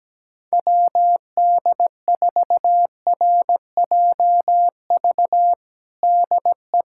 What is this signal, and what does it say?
Morse code, heard as a keyed tone, about 17 wpm, 700 Hz. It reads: WD4RJV DE
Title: USA Callsign DE Callsign - 6 Characters - 17wpm